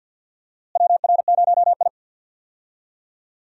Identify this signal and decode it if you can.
Morse code, heard as a keyed tone, about 52 wpm, 700 Hz. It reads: WR0I
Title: USA Call Signs - 4 Characters - Repeated 3x - 50wpm